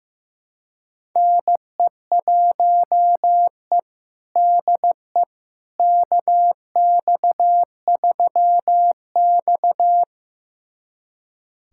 Morse code, heard as a keyed tone, about 15 wpm, 700 Hz. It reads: NE1E DE KX3X
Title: USA Callsign DE Callsign - 4 Characters - 15wpm